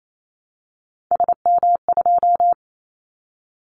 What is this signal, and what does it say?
Morse code, heard as a keyed tone, about 28 wpm, 700 Hz. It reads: SM2
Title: Callsign Prefixes for Top 25 DXCC - Repeated 3x - 28wpm